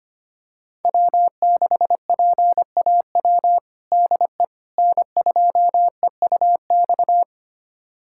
Morse code, heard as a keyed tone, about 25 wpm, 700 Hz. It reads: W6PAW DE N2EUX
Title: USA Callsign DE Callsign - 5 Characters - 25wpm